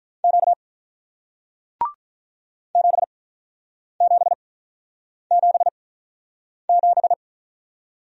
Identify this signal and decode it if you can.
Morse code, heard as a keyed tone, about 48 wpm, spaced as for 9 wpm, 700 Hz. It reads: Q 7777